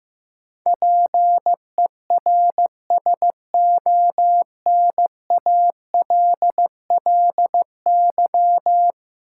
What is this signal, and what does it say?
Morse code, heard as a keyed tone, about 15 wpm, 700 Hz. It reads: PERSONALLY
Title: Sentences from Top 1400 Words - New Word Drill - 15wpm